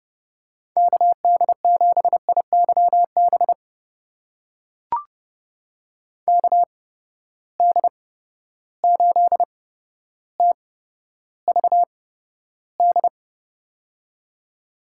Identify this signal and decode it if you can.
Morse code, heard as a keyed tone, about 30 wpm, spaced as for 13 wpm, 700 Hz. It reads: KD7IYB KD8TVD
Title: USA Call Signs - 6 Characters - 8x Character Spacing - 30wpm